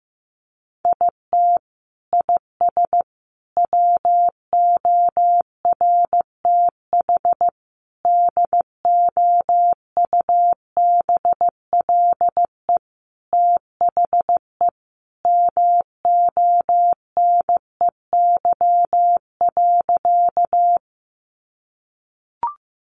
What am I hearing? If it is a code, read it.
Morse code, heard as a keyed tone, about 15 wpm, 700 Hz. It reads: IT IS WORTH DOUBLE THE MONEY.
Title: Sentences from Top 1400 Words 15wpm